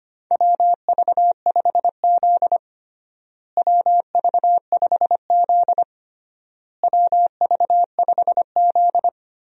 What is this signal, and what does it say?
Morse code, heard as a keyed tone, about 25 wpm, 700 Hz. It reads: WV5Z WV5Z WV5Z
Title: USA Call Signs - 4 Characters - Repeated 3x - 25wpm